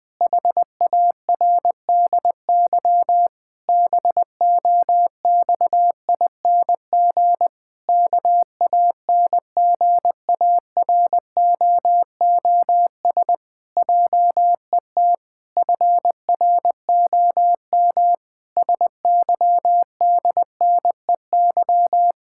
Morse code, heard as a keyed tone, about 20 wpm, 700 Hz. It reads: HARDY BOXING KANGAROOS JET FROM SYDNEY
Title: Pangram Challenge v2 - 20wpm